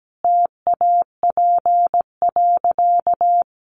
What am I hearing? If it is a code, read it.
Morse code, heard as a keyed tone, about 17 wpm, 700 Hz. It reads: TAP.
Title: Sentences from Top 1400 Words 17wpm